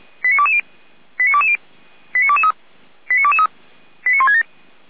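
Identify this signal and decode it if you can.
Five-tone selective call (ZVEI), heard as quick runs of five tones, tones of about 70 ms. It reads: 89220 89220 89223 89223 89167
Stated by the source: ZVEI five-tone selective call received via SDR, archived by Feuerwehr Orlamünde.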